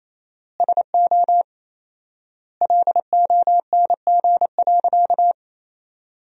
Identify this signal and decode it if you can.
Morse code, heard as a keyed tone, about 28 wpm, 700 Hz. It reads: SO LONG.
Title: 4x Word-Spacing - Sentences from Top 100 Words - 28wpm